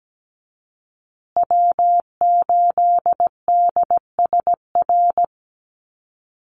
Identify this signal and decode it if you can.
Morse code, heard as a keyed tone, about 17 wpm, 700 Hz. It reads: W8DSR
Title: USA Call Signs Repeated 3x 17wpm